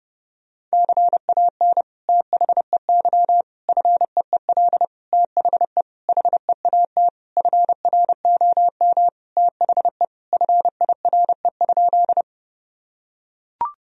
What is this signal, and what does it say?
Morse code, heard as a keyed tone, about 30 wpm, 700 Hz. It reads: CAN THEY FEEL THE HEAT FROM THE FIRE?
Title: Sentences from Top 1000 Words - Review Part 3 - 30wpm